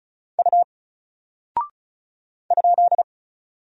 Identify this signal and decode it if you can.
Morse code, heard as a keyed tone, about 35 wpm, 700 Hz. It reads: U ?
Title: Single Character (Letter:Number:Punctuation:Prosign) Alt 35wpm